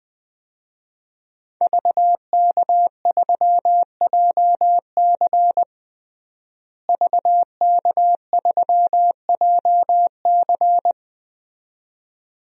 Morse code, heard as a keyed tone, about 21 wpm, 700 Hz. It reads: VK3JC VK3JC
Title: DXCC-100 Call Signs - 5 Characters - Repeated 2x - 20wpm